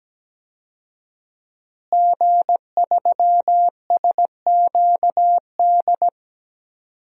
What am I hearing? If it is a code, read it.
Morse code, heard as a keyed tone, about 17 wpm, 700 Hz. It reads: G3SQD
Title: DXCC-100 Call Signs - 5 Characters - Repeated 3x - 17wpm